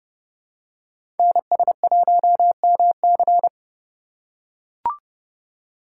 Morse code, heard as a keyed tone, about 30 wpm, 700 Hz. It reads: NS1MC
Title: USA Call Signs - 5 Characters - 2x Character Spacing - 30wpm